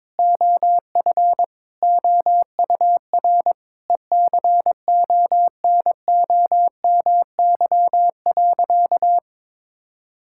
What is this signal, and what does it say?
Morse code, heard as a keyed tone, about 22 wpm, 700 Hz. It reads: OF OUR ECONOMY.
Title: Sentences from Top 1000 Words - Review Part 5 - 22wpm